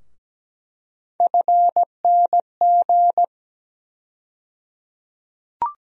Morse code, heard as a keyed tone, about 17 wpm, 700 Hz. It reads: FNG
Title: USA Call Sign Postfixes - 3 Letters 17wpm